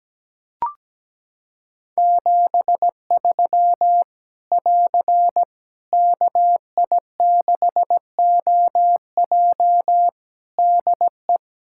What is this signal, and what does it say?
Morse code, heard as a keyed tone, about 17 wpm, 700 Hz. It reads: 73 <AR> KI6OJ DE
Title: K1USN SST QSO Practice 17wpm